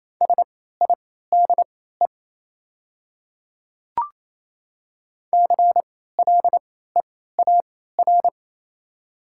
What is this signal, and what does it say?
Morse code, heard as a keyed tone, about 27 wpm, spaced as for 16 wpm, 700 Hz. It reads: SIDE CLEAR
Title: 3x Character Spacing - Top 500 Words - 28wpm